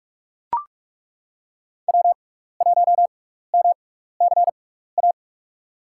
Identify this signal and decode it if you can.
Morse code, heard as a keyed tone, about 42 wpm, spaced as for 15 wpm, 700 Hz. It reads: W1MCA
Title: USA Call Signs - 6x Character Spacing - 45wpm